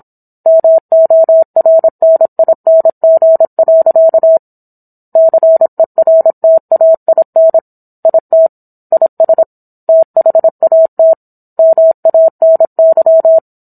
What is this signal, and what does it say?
Morse code, heard as a keyed tone, about 27 wpm, 650 Hz. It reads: MORNING. CERTAIN IT IS THAT MANY
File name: Book One / 17-Chapter 17 - 26 WPM